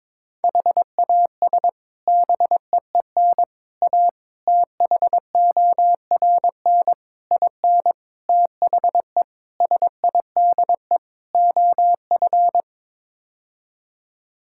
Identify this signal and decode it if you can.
Morse code, heard as a keyed tone, about 22 wpm, 700 Hz. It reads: HAS BEEN A THORN IN THE SIDE OF
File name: practice / 22wpm / Sets of 8 Words - 22wpm